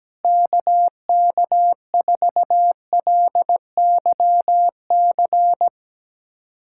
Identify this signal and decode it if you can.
Morse code, heard as a keyed tone, about 17 wpm, 700 Hz. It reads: KK4LYC
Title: USA Callsign DE Callsign - 6 Characters - 17wpm